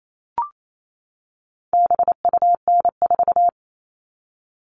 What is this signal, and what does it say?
Morse code, heard as a keyed tone, about 28 wpm, 700 Hz. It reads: BUN4
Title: All Cut Numbers - 4-Digit 28wpm